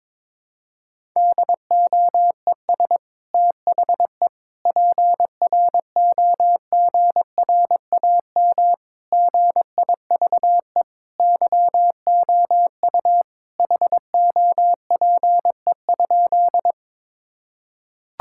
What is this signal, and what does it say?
Morse code, heard as a keyed tone, about 22 wpm, 700 Hz. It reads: DOES THE PROGRAM GIVE YOU HOPE?
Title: Sentences from Top 1000 Words - Review Part 5 - 22wpm